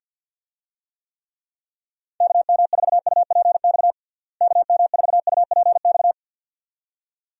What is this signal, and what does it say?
Morse code, heard as a keyed tone, about 51 wpm, 700 Hz. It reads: KM4UPX KM4UPX
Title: USA Call Signs - 6 Characters - Repeated 2x - 50wpm